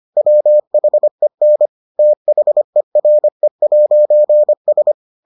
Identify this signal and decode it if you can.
Morse code, heard as a keyed tone, about 25 wpm, 600 Hz. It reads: WHEN THERE'S